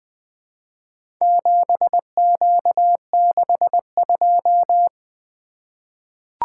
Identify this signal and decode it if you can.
Morse code, heard as a keyed tone, about 20 wpm, 700 Hz. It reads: 7Q62
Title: Four Letter-Number Combinations - 20wpm